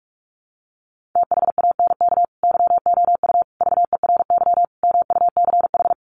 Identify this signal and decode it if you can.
Morse code, heard as a keyed tone, about 44 wpm, 700 Hz. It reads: THANK YOU VERY MUCH
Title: Sets of 5 Words 45wpm